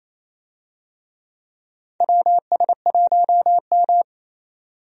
Morse code, heard as a keyed tone, about 28 wpm, 700 Hz. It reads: WS1M
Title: USA Call Signs - 4 Characters - 28wpm